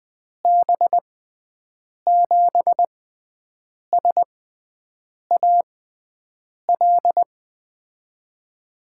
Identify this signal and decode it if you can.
Morse code, heard as a keyed tone, about 21 wpm, spaced as for 7 wpm, 700 Hz. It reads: B7SAL
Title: USA Call Signs - 6x Character Spacing - 20wpm